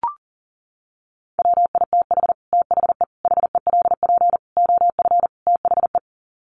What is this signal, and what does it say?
Morse code, heard as a keyed tone, about 40 wpm, 700 Hz. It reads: WITH THE HELP OF THE
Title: Sets of 5 Words Encore 40wpm